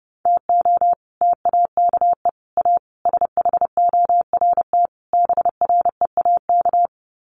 Morse code, heard as a keyed tone, about 30 wpm, 700 Hz. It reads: TO TAKE A SHORT BREAK
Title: Sets of 5 Words Encore2 30wpm